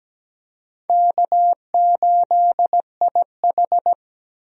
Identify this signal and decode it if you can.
Morse code, heard as a keyed tone, about 17 wpm, 700 Hz. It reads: K8IH
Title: USA Call Signs - 4 Characters - Repeated 2x - 17wpm